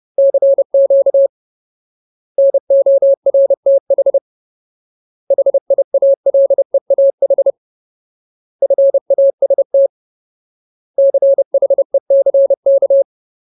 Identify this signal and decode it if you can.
Morse code, heard as a keyed tone, about 30 wpm, 550 Hz. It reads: CQ NORTH HIALEAH FAST CHECK
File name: Ultimate Word List / 30WPM Ultimate